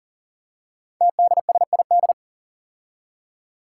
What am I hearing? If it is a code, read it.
Morse code, heard as a keyed tone, about 40 wpm, 700 Hz. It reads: TDSID
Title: Five Challenging Letter-Number Combinations 40wpm